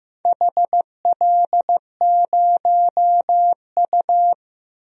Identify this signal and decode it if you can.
Morse code, heard as a keyed tone, about 15 wpm, 700 Hz. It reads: HL0U
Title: DXCC-100 Call Signs - 4 Characters - 15wpm